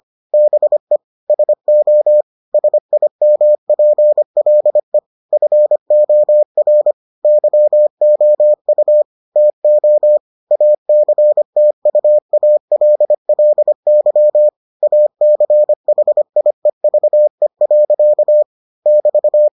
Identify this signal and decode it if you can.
Morse code, heard as a keyed tone, about 25 wpm, 600 Hz. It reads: BE SO SIMPLE FOR YOU TO ACTUALLY ACHIEVE. <BT>